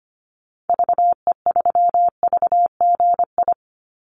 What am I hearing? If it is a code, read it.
Morse code, heard as a keyed tone, about 25 wpm, 700 Hz. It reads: VE3VGI
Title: Call Signs Repeated 2x - 25wpm